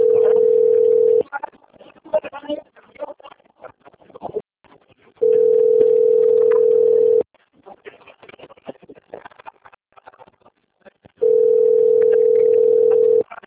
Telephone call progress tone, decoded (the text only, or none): ringback tone